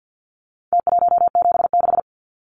Morse code, heard as a keyed tone, about 50 wpm, 700 Hz. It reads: T176
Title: Numbers - Contest Style - 4-Digit 50wpm